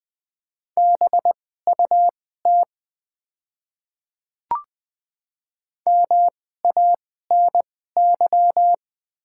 Morse code, heard as a keyed tone, about 20 wpm, spaced as for 14 wpm, 700 Hz. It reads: BUT MANY
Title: 2x Character Spacing - Top 100 Words - 20wpm